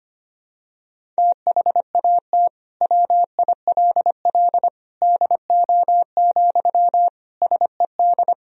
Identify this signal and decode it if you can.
Morse code, heard as a keyed tone, about 25 wpm, 700 Hz. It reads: THAT WILL DO, SED
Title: Sentences from Top 100 Words and Abbreviations 25wpm